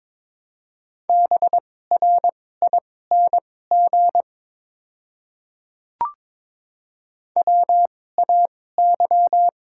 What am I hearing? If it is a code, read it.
Morse code, heard as a keyed tone, about 22 wpm, spaced as for 16 wpm, 700 Hz. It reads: BRING WAY
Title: 2x Character Spacing - Top 500 Words - 22wpm